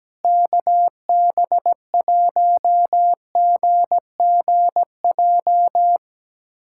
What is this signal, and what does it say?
Morse code, heard as a keyed tone, about 17 wpm, 700 Hz. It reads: KB1GGJ
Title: USA Callsign DE Callsign - 6 Characters - 17wpm